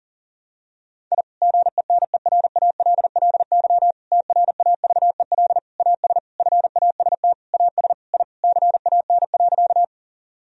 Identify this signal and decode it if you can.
Morse code, heard as a keyed tone, about 40 wpm, 700 Hz. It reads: I GENERALLY TRAVEL AS FAST AS I CAN.
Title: Sentences from Top 1000 Words - Review Part 4 - 40wpm